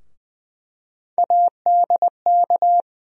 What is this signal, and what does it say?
Morse code, heard as a keyed tone, about 20 wpm, 700 Hz. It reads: ADK
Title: USA Call Sign Postfixes - 3 Letters 20wpm